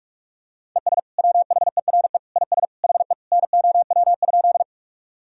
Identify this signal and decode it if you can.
Morse code, heard as a keyed tone, about 42 wpm, 700 Hz. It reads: ES WHERE IS HE NOW?
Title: Sentences from Top 200 Words and Abbreviations 45wpm